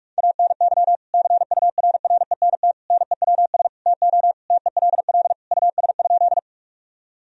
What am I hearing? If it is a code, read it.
Morse code, heard as a keyed tone, about 42 wpm, 700 Hz. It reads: ANY CURRENT NEWS TO TELL US?
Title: Sentences from Top 1000 Words - Review Part 4 - 45wpm